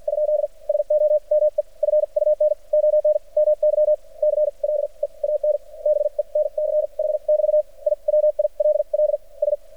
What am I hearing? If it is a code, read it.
Morse code, heard as a keyed tone, about 56 wpm, spaced as for 42 wpm, 600 Hz. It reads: ? SOME FUN ON MY CLEAN BENCH <BT> I WILLS
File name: archived / CRSnet